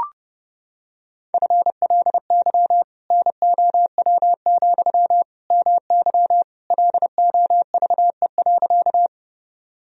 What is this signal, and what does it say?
Morse code, heard as a keyed tone, about 32 wpm, 700 Hz. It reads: FLY NOW, MY LOVE.
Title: Sentences from Top 1000 Words - Review Part 2 - 30wpm